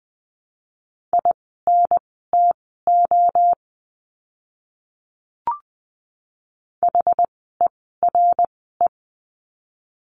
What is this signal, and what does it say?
Morse code, heard as a keyed tone, about 19 wpm, spaced as for 14 wpm, 700 Hz. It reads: INTO HERE
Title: 160 Lesson 26f - Letter g - Words 20wpm